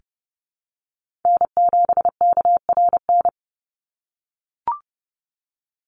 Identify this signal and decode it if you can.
Morse code, heard as a keyed tone, about 30 wpm, 700 Hz. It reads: N7KRN
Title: NA SOTA Chaser Callsigns 30wpm